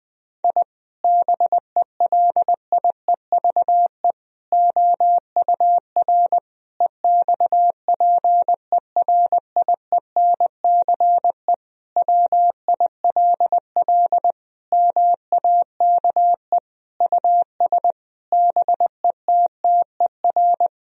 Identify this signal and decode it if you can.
Morse code, heard as a keyed tone, about 20 wpm, 700 Hz. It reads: I BELIEVE OUR EXPERIENCE WILL MAKE US BETTER
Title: Sentences from Top 1000 Words - Review Part 3 - 20wpm